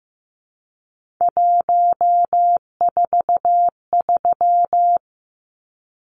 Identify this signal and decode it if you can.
Morse code, heard as a keyed tone, about 15 wpm, 700 Hz. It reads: 143
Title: Numbers - 3-Digit 15wpm